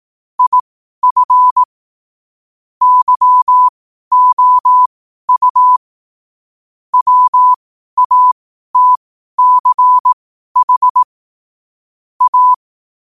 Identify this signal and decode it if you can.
Morse code, heard as a keyed tone, about 18 wpm, spaced as for 12 wpm, 1000 Hz. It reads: IF YOU WATCH A
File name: qotdfiles / 12wpm